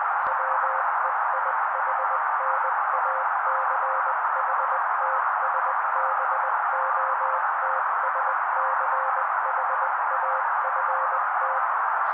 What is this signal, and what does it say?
Morse code, heard as a keyed tone, about 20 wpm, 530 Hz. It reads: WEIHNACHTSBOTSCHAFT
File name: daily / 18wpm noise